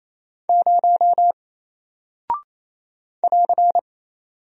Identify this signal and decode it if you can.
Morse code, heard as a keyed tone, about 28 wpm, 700 Hz. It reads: 0 <AR>
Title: Single Character (Letter:Number:Punctuation:Prosign) Alt 28wpm